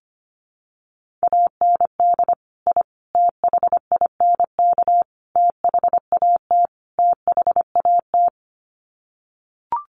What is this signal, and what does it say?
Morse code, heard as a keyed tone, about 25 wpm, 700 Hz. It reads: AND I THINK THAT THAT
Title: Sets of 5 Words Encore 25wpm